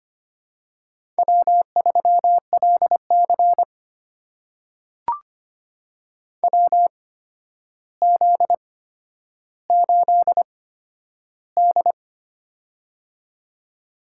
Morse code, heard as a keyed tone, about 25 wpm, spaced as for 11 wpm, 700 Hz. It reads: W3LC WZ8D
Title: USA Call Signs - 4 Characters - 8x Character Spacing - 25wpm